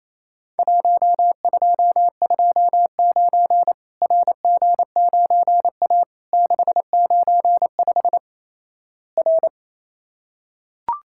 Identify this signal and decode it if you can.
Morse code, heard as a keyed tone, about 28 wpm, 700 Hz. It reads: 1229 RG9A 695 R
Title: Worked All Europe DX Contest - QTC Exchange - 28wpm